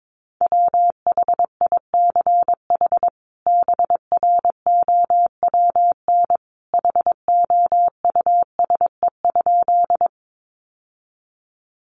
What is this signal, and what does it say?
Morse code, heard as a keyed tone, about 22 wpm, 700 Hz. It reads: WHICH BROWN HOUSE?
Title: Sentences from Top 1200 Words 22wpm